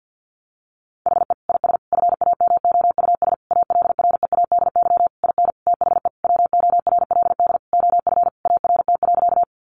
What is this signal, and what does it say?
Morse code, heard as a keyed tone, about 50 wpm, 700 Hz. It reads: HE IS FAMOUS ALREADY IN THE WORLD OF ART.
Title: Sentences from Top 1500 Words 50wpm